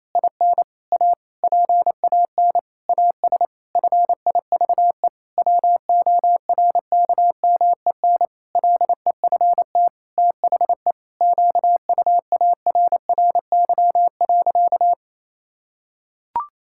Morse code, heard as a keyed tone, about 28 wpm, 700 Hz. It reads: IN A PAN AS FIVE WORKMEN LEFT THE QUARRY.
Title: Pangram Challenge v2 - 28wpm